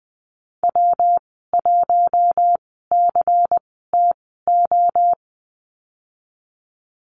Morse code, heard as a keyed tone, about 20 wpm, spaced as for 14 wpm, 700 Hz. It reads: W1CTO
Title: 122 Lesson 20g - Number 3 - Call Signs 20wpm